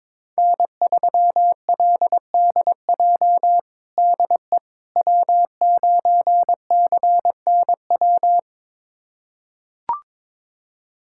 Morse code, heard as a keyed tone, about 22 wpm, 700 Hz. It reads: N3LDJ DE W9CNW
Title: USA Callsign DE Callsign - 5 Characters - 22wpm